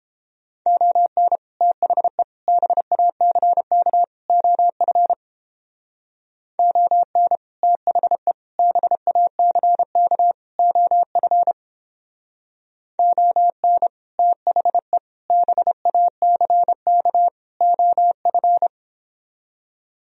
Morse code, heard as a keyed tone, about 29 wpm, 700 Hz. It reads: ON THE BACK OF ON THE BACK OF ON THE BACK OF